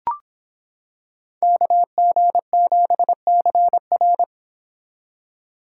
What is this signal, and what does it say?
Morse code, heard as a keyed tone, about 26 wpm, 700 Hz. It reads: KG7CR